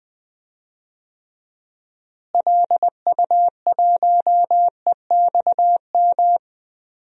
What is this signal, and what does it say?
Morse code, heard as a keyed tone, about 21 wpm, 700 Hz. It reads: LU1EXM
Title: DXCC-100 Call Signs - 6 Characters - Repeated 3x - 20wpm